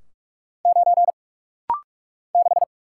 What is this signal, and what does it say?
Morse code, heard as a keyed tone, about 44 wpm, 700 Hz. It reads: 9 6
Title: Single Letter-Number - Rapid-Fire 45wpm